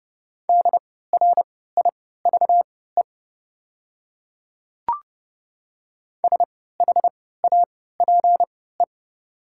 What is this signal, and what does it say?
Morse code, heard as a keyed tone, about 30 wpm, spaced as for 17 wpm, 690 Hz. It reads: DRIVE SHAPE
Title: 3x Character Spacing - Top 500 Words - 30wpm